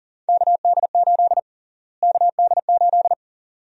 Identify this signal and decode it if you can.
Morse code, heard as a keyed tone, about 40 wpm, 700 Hz. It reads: KD8 KD8
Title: USA Call Sign Prefixes - Repeated 2x - 40wpm